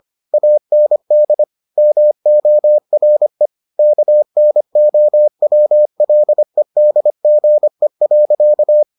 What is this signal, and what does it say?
Morse code, heard as a keyed tone, about 25 wpm, 600 Hz. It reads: AND MORE KNOWLEDGE.